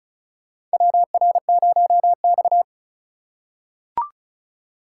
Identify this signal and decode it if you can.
Morse code, heard as a keyed tone, about 36 wpm, 700 Hz. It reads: WR0X
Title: USA Call Signs - 4 Characters - 2x Character Spacing - 35wpm